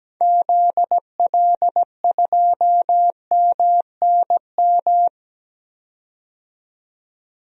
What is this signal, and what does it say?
Morse code, heard as a keyed tone, about 17 wpm, 700 Hz. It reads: ZL2MNM
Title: DXCC-100 Call Signs - 6 Characters - 17wpm